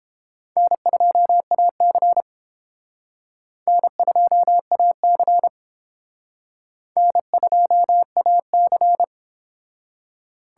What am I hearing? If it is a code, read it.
Morse code, heard as a keyed tone, about 29 wpm, 700 Hz. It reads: N2AC N2AC N2AC